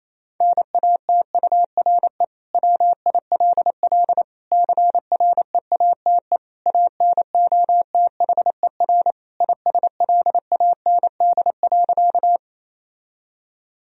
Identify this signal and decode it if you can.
Morse code, heard as a keyed tone, about 27 wpm, 700 Hz. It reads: NATURE WILL CREATE ANOTHER ISLAND.
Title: Sentences from Top 1000 Words - Review Part 3 - 28wpm